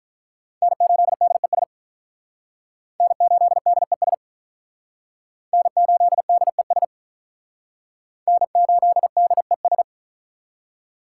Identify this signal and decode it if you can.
Morse code, heard as a keyed tone, about 44 wpm, 700 Hz. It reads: N8DES N8DES N8DES N8DES